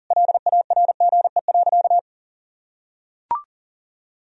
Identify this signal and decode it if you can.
Morse code, heard as a keyed tone, about 37 wpm, 700 Hz. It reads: LARGE.
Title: Sentences from Top 1000 Words - Review Part 1 - 40wpm